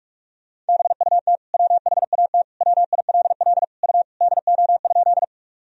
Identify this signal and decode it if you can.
Morse code, heard as a keyed tone, about 46 wpm, 700 Hz. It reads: BUT WHAT WILL U DO?
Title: Sentences from Top 100 Words and Abbreviations 45wpm